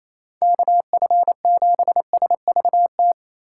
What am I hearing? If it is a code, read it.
Morse code, heard as a keyed tone, about 35 wpm, 700 Hz. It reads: KF7SVT
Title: Speed Racing - USA 2x3 Call Signs 28wpm